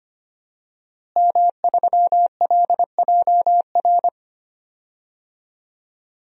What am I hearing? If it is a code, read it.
Morse code, heard as a keyed tone, about 25 wpm, 700 Hz. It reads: M3LJR
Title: DXCC-100 Call Signs - 5 Characters - Repeated 3x - 25wpm